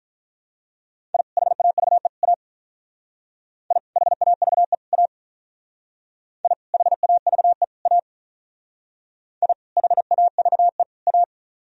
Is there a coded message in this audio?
Yes: Morse code from 1.1 to 11.2 s, heard as a keyed tone, about 44 wpm, 700 Hz. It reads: I HAVE A I HAVE A I HAVE A I HAVE A